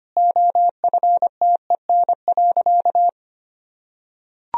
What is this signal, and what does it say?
Morse code, heard as a keyed tone, about 25 wpm, 700 Hz. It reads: OFTEN.
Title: Sentences from Top 1000 Words - Review Part 1 - 25wpm